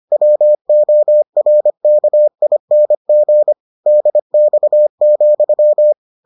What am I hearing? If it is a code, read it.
Morse code, heard as a keyed tone, about 25 wpm, 600 Hz. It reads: WORKING DX,